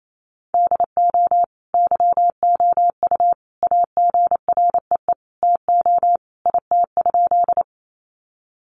Morse code, heard as a keyed tone, about 28 wpm, 700 Hz. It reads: DO YOU AGREE TO IT?
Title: Sentences from Top 1100 Words 28wpm